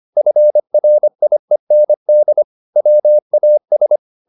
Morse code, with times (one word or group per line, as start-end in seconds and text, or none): 0.2-2.4 FRIEND
2.8-4.0 WAS